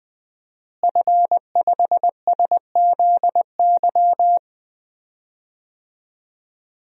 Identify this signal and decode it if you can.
Morse code, heard as a keyed tone, about 20 wpm, 700 Hz. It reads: F5SZY
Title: DXCC-100 Call Signs - 5 Characters - Repeated 3x - 20wpm